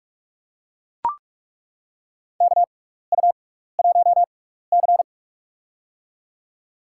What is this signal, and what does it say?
Morse code, heard as a keyed tone, about 41 wpm, spaced as for 15 wpm, 700 Hz. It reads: KU1C
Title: USA Call Signs - 4 Characters - 6x Character Spacing - 45wpm